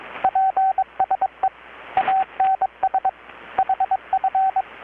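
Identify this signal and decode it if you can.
Morse code, heard as a keyed tone, about 21 wpm, 760 Hz, with radio noise. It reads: PSE ANS HF
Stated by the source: Morse from KPH's Night of Nights XX maritime radiotelegraphy event, received in Chiapas, Mexico about 3,800 km away on a Kenwood R-600 with a grounded 10 m long-wire antenna.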